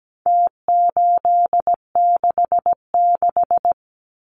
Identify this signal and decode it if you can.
Morse code, heard as a keyed tone, about 17 wpm, 700 Hz. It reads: T866
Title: Numbers - Contest Style - 4-Digit 17wpm